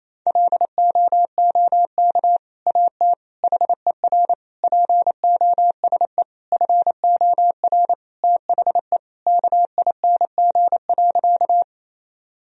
Morse code, heard as a keyed tone, about 28 wpm, 700 Hz. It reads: LOOK AT HER POSE FOR THE KING.
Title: Sentences from Top 1000 Words - Review Part 2 - 28wpm